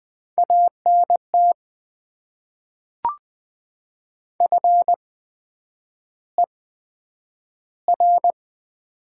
Morse code, heard as a keyed tone, about 20 wpm, spaced as for 9 wpm, 700 Hz. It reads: ANT FER
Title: Top 20 QSO Elements - 8x Character Spacing - 20wpm